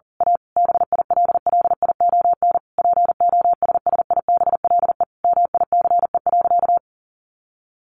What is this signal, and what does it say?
Morse code, heard as a keyed tone, about 40 wpm, 700 Hz. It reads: A BILLION POSSIBLE MICE.
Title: A Sound Of Thunder 40wpm